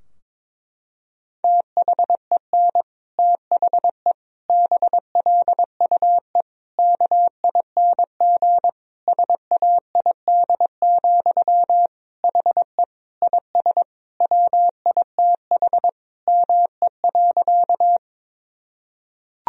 Morse code, heard as a keyed tone, about 22 wpm, 700 Hz. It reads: THEN THE BLUE KING SAID, HE IS WITH ME.
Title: Sentences from Top 1000 Words - Review Part 3 - 22wpm